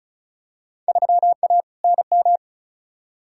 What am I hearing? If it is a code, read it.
Morse code, heard as a keyed tone, about 34 wpm, 700 Hz. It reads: 3A NM
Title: Field Day Exchanges Encore - 35wpm